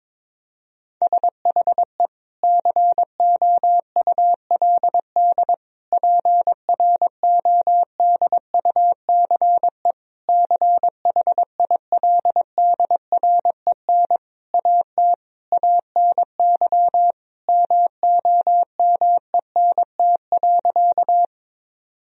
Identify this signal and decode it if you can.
Morse code, heard as a keyed tone, about 21 wpm, 700 Hz. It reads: SHE COULD PRODUCE CHILDREN AT ANY MOMENT.
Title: Sentences from Top 1000 Words - Review Part 3 - 22wpm